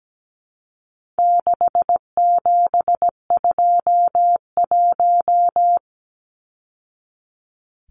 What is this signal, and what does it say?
Morse code, heard as a keyed tone, about 17 wpm, 700 Hz. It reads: 6721
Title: Numbers - 4-Digit 17wpm